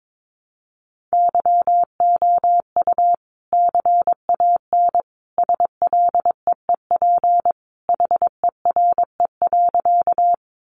Morse code, heard as a keyed tone, about 22 wpm, 700 Hz. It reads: YOU CAN SLEEP HERE.
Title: Sentences from Top 500 Words 22wpm